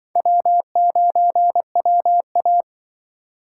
Morse code, heard as a keyed tone, about 23 wpm, 700 Hz. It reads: W9WA
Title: Speed Racing - USA Call Signs 20wpm